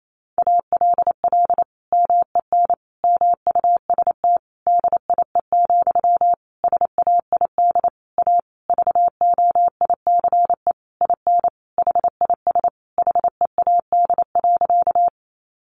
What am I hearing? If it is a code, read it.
Morse code, heard as a keyed tone, about 28 wpm, 700 Hz. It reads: ALL MEN MUST DIE, SAID A VOICE IN HIS HEAD.
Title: Sentences from Top 1300 Words 28wpm